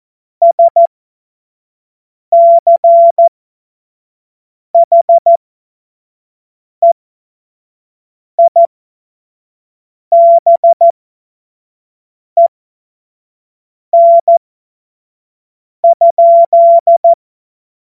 Morse code, heard as a keyed tone, about 14 wpm, spaced as for 5 wpm, 680 Hz. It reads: SCHEIBEN?
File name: Morsekurs / Witze / Witz 12. Oktober 2021, 25 F70